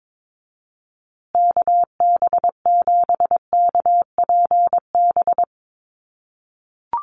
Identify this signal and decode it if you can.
Morse code, heard as a keyed tone, about 22 wpm, 700 Hz. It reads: KB7KPB
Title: USA Call Signs Repeated 3x 22wpm